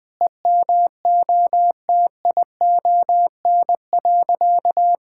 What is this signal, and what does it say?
Morse code, heard as a keyed tone, about 20 wpm, 700 Hz. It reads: EMOTION.
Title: Sentences from Top 1600 Words 20wpm